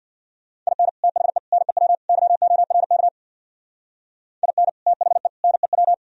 Morse code, heard as a keyed tone, about 53 wpm, 700 Hz. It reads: IN THE NEW YORK IN THE NEW
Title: Speed-Racing - Sets of 4 Words 50wpm